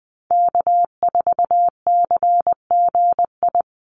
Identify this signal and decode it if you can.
Morse code, heard as a keyed tone, about 20 wpm, 700 Hz. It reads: K4CGI
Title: USA Call Signs 20wpm